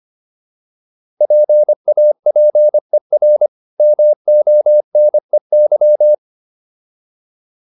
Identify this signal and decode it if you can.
Morse code, heard as a keyed tone, about 25 wpm, 600 Hz. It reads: PAPER MONEY